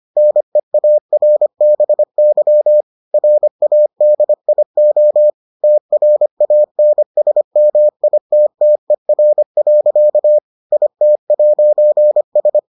Morse code, heard as a keyed tone, about 25 wpm, 600 Hz. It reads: NEARBY RADIO TRANSMITTER. IT'S